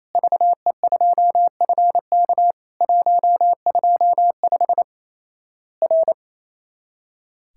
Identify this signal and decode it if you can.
Morse code, heard as a keyed tone, about 28 wpm, 700 Hz. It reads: VE2FK 125 R
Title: Worked All Europe DX Contest - QTC Exchange - 28wpm